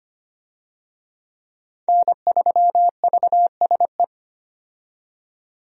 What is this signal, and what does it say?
Morse code, heard as a keyed tone, about 25 wpm, 700 Hz. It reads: N3VSE
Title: USA Call Signs - 5 Characters - 25wpm